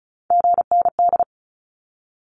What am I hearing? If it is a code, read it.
Morse code, heard as a keyed tone, about 34 wpm, 700 Hz. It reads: GND
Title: Abbreviations 35wpm